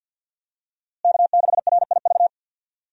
Morse code, heard as a keyed tone, about 52 wpm, 700 Hz. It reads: K6RIV
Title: USA Call Signs - 5 Characters - Repeated 3x - 50wpm